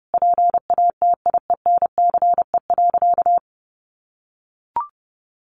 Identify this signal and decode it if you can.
Morse code, heard as a keyed tone, about 30 wpm, 700 Hz. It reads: PATIENCE.
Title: Sentences from Top 1300 Words 30wpm